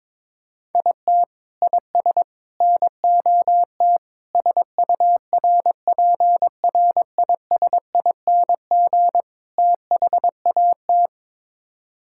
Morse code, heard as a keyed tone, about 22 wpm, 700 Hz. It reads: IT IS NOT SURPRISING THAT
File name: practice / 22wpm / Build A Phrase - 5-Word Phrases - 22wpm